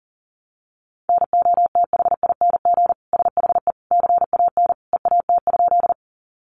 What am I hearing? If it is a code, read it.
Morse code, heard as a keyed tone, about 37 wpm, 700 Hz. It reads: NOTHING SHE CAN EAT?
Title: Sentences from Top 500 Words 40wpm